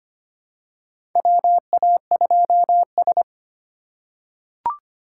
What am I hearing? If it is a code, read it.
Morse code, heard as a keyed tone, about 25 wpm, 700 Hz. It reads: WA2S
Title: USA Call Signs - 4 Characters - 4x Character Spacing - 25wpm